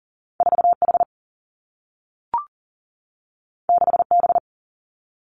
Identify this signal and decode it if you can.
Morse code, heard as a keyed tone, about 42 wpm, 700 Hz. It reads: 4 H 6 B
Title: Two Challenging Letter-Number Combinations 40wpm